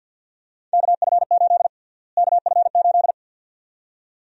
Morse code, heard as a keyed tone, about 46 wpm, 700 Hz. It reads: KF8 KF8
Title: USA Call Sign Prefixes - Repeated 2x - 50wpm